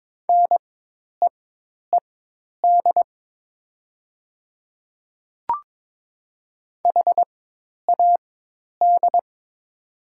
Morse code, heard as a keyed tone, about 22 wpm, spaced as for 10 wpm, 700 Hz. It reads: NEED HAD
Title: 4x Character Spacing - Top 200 Words - 22wpm